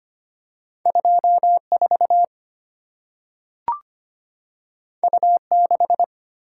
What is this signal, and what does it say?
Morse code, heard as a keyed tone, about 25 wpm, 700 Hz. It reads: 24 U6
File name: Two Letter-Number Combinations - 25wpm